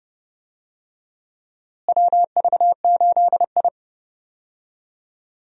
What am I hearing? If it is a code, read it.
Morse code, heard as a keyed tone, about 31 wpm, 700 Hz. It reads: WV8I
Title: USA Call Signs - 4 Characters - Repeated 3x - 30wpm